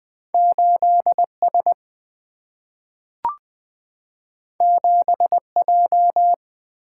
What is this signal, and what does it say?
Morse code, heard as a keyed tone, about 20 wpm, 700 Hz. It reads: 8S 7J
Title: Single-Letter Single-Number 20wpm